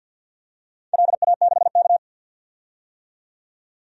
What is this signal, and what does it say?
Morse code, heard as a keyed tone, about 45 wpm, 700 Hz. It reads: LA6K
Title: DXCC-100 Call Signs - 4 Characters - 50wpm